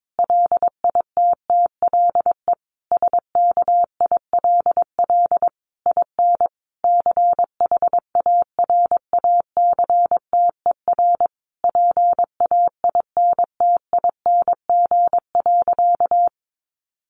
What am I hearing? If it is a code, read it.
Morse code, heard as a keyed tone, about 22 wpm, 700 Hz. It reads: LITTLE SKILL IN CHARACTER PAINTING.
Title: Sentences from Top 1200 Words 22wpm